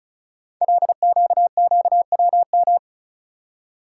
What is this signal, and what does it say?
Morse code, heard as a keyed tone, about 36 wpm, 690 Hz. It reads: LQQWM
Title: Code Groups - 5 Letters - 35wpm